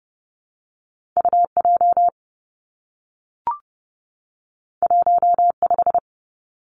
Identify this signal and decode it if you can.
Morse code, heard as a keyed tone, about 30 wpm, 700 Hz. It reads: UJ 15
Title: Two Challenging Letter-Number Combinations 30wpm